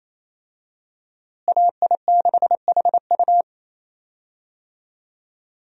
Morse code, heard as a keyed tone, about 28 wpm, 700 Hz. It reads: AI6HU
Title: USA Call Signs - 5 Characters - Repeated 2x - 28wpm